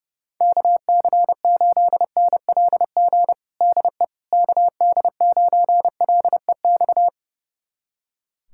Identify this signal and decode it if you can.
Morse code, heard as a keyed tone, about 30 wpm, 700 Hz. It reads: KC8NLG DE KD9LEX
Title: USA Callsign DE Callsign - 6 Characters - 30wpm